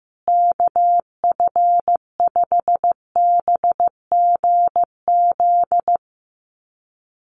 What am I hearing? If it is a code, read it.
Morse code, heard as a keyed tone, about 15 wpm, 700 Hz. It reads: KF5BGZ
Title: USA Call Signs Repeated 3x 15wpm